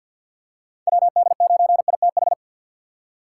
Morse code, heard as a keyed tone, about 48 wpm, 700 Hz. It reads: WD9ITH
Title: USA Call Signs - 6 Characters - 3x Character Spacing - 50wpm